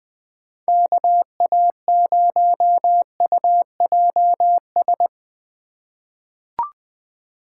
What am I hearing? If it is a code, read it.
Morse code, heard as a keyed tone, about 20 wpm, 700 Hz. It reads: KA0UJS
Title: USA Call Signs - 6 Characters - 4x Character Spacing - 20wpm